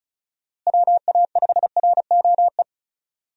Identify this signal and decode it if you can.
Morse code, heard as a keyed tone, about 36 wpm, 700 Hz. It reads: WA5ROE
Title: USA Call Signs - 6 Characters - Repeated 3x - 35wpm